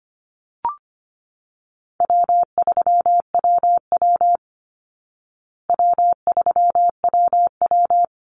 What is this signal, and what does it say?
Morse code, heard as a keyed tone, about 25 wpm, 700 Hz. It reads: W3WW W3WW
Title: Call Signs Repeated 3x - 25wpm